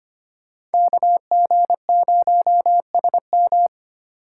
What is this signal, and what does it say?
Morse code, heard as a keyed tone, about 33 wpm, 700 Hz. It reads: KG0SM